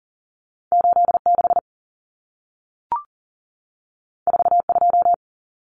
Morse code, heard as a keyed tone, about 40 wpm, 700 Hz. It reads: 86 42
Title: Common Cut Numbers - 2-Digit 40wpm